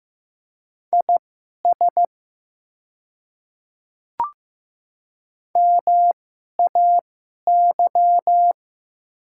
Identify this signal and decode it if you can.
Morse code, heard as a keyed tone, about 15 wpm, spaced as for 11 wpm, 700 Hz. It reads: IS MAY
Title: 2x Character Spacing - Top 100 Words - 15wpm